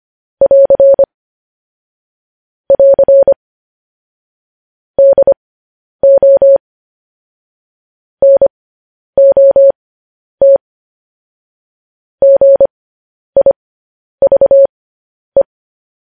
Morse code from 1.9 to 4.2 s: <AR>

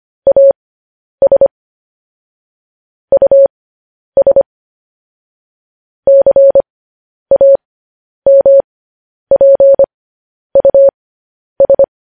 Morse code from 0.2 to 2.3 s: AS

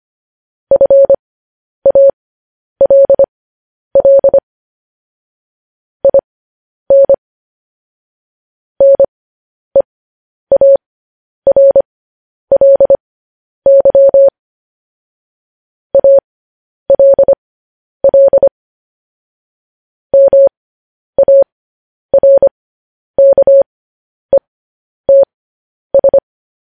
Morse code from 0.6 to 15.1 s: FALL IN NEARLY

